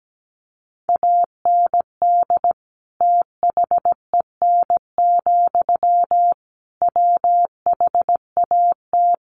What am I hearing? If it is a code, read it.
Morse code, heard as a keyed tone, about 17 wpm, 700 Hz. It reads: AND THEN, WHAT
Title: Sentences from Top 1500 Words 17wpm